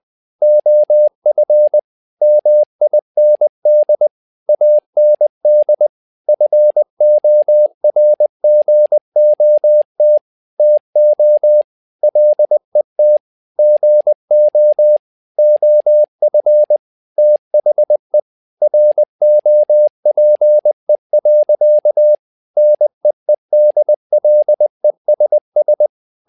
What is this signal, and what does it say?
Morse code, heard as a keyed tone, about 20 wpm, 600 Hz. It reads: OF MIND AND FORGOT TO LET GO OF THE ROPE. NEEDLESS
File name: learning center / tower trouble / tower trouble 20wpm